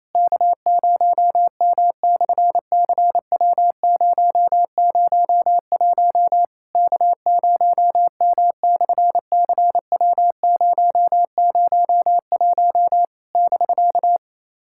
Morse code, heard as a keyed tone, about 28 wpm, 700 Hz. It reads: K0M/CW001 K0M/CW001 <BK>
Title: SOTA QSO Practice 28wpm